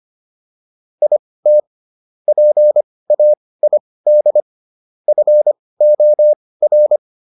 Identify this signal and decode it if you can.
Morse code, heard as a keyed tone, about 25 wpm, spaced as for 18 wpm, 600 Hz. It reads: IT PAID FOR